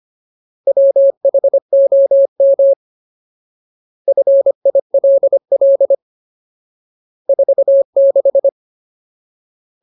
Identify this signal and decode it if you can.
Morse code, heard as a keyed tone, about 25 wpm, 550 Hz. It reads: WHOM FILL 46
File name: Ultimate Word List / 25WPM Ultimate